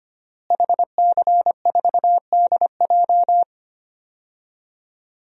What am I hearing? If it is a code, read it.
Morse code, heard as a keyed tone, about 25 wpm, 700 Hz. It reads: HC4DJ
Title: DXCC-100 Call Signs - 5 Characters - Repeated 3x - 25wpm